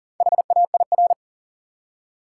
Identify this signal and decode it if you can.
Morse code, heard as a keyed tone, about 40 wpm, 700 Hz. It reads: HAIR
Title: Four Letter Words - 40wpm